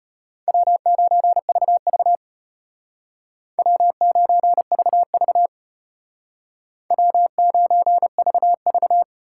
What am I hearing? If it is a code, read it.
Morse code, heard as a keyed tone, about 32 wpm, 700 Hz. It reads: W9VV W9VV W9VV